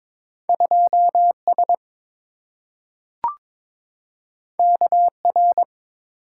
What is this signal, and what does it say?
Morse code, heard as a keyed tone, about 22 wpm, 690 Hz. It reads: 2S KR
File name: practice / 22wpm / Two Letter-Number Combinations - 22wpm